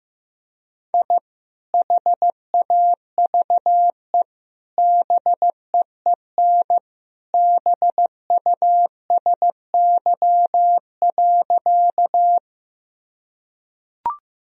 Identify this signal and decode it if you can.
Morse code, heard as a keyed tone, about 15 wpm, 700 Hz. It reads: I HAVE BEEN BUSY.
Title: Sentences from Top 1000 Words - Review Part 3 - 15wpm